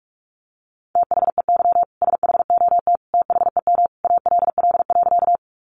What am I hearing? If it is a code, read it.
Morse code, heard as a keyed tone, about 44 wpm, 700 Hz. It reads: THEY SHOT THEM ALL.
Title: Sentences from Top 1100 Words 45wpm